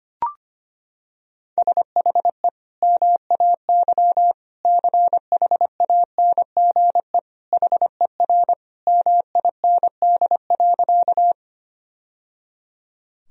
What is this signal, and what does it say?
Morse code, heard as a keyed tone, about 25 wpm, 700 Hz. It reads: SHE MAY CHANGE HER MIND.
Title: Sentences from Top 1000 Words - Review Part 1 - 25wpm